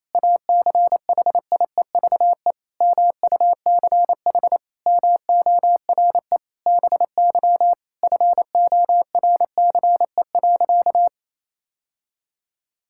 Morse code, heard as a keyed tone, about 28 wpm, 700 Hz. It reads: ACHIEVE MUCH MORE BY FORCE.
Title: Sentences from Top 1000 Words - Review Part 6 - 28wpm